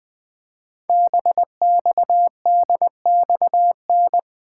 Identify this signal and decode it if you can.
Morse code, heard as a keyed tone, about 20 wpm, 700 Hz. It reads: BXDXN
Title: Code Groups - 5 Letter:Numbers - 20wpm